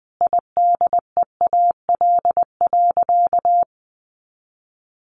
Morse code, heard as a keyed tone, about 20 wpm, 700 Hz. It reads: IDEAL.
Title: Sentences from Top 1400 Words 20wpm